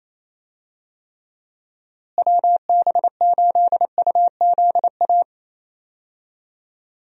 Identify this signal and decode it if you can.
Morse code, heard as a keyed tone, about 27 wpm, 700 Hz. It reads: WB8UZA
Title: USA Call Signs - 6 Characters - Repeated 2x - 28wpm